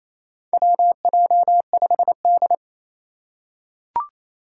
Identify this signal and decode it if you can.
Morse code, heard as a keyed tone, about 28 wpm, 700 Hz. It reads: WJ5D
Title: USA Call Signs - 2x Character Spacing - 28wpm